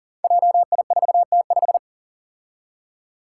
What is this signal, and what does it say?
Morse code, heard as a keyed tone, about 40 wpm, 700 Hz. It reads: JI4T5
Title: Five Challenging Letter-Number Combinations 40wpm